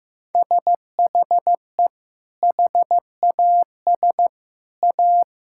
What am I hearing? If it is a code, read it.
Morse code, heard as a keyed tone, about 14 wpm, 700 Hz. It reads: SHE HAS A
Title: Sentences from Top 1000 Words - Review Part 6 - 15wpm